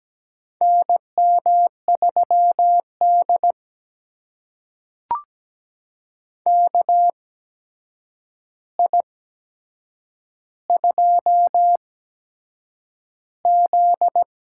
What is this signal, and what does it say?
Morse code, heard as a keyed tone, about 17 wpm, spaced as for 7 wpm, 700 Hz. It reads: NM3D KI2Z
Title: USA Call Signs - 4 Characters - 8x Character Spacing - 17wpm